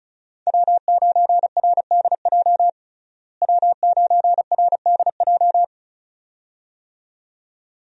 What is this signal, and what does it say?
Morse code, heard as a keyed tone, about 34 wpm, 700 Hz. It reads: W9RDJ W9RDJ
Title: USA Call Signs - 5 Characters - Repeated 2x - 35wpm